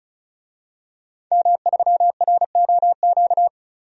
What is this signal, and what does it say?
Morse code, heard as a keyed tone, about 36 wpm, 700 Hz. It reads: M3ROQ
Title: DXCC-100 Call Signs - 5 Characters - 35wpm